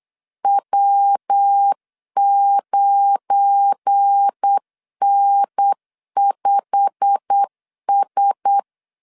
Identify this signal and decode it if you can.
Morse code, heard as a keyed tone, about 8 wpm, 800 Hz. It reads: W9N5S
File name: LEZIONI / 8 YJ45 / 8 YJ45 8wpm 3214